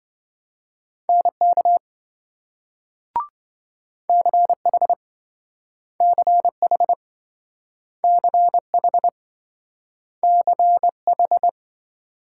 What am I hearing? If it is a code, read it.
Morse code, heard as a keyed tone, about 27 wpm, 700 Hz. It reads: NK CH CH CH CH